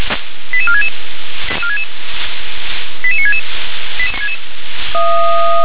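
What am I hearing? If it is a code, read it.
Five-tone selective call (ZVEI), heard as quick runs of five tones, tones of about 70 ms. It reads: 88466 88966
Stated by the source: ZVEI five-tone call 88960 received by SDR.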